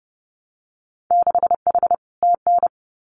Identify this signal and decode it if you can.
Morse code, heard as a keyed tone, about 30 wpm, 700 Hz. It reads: 6H TN
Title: Winter Field Day Exchanges Encore 30wpm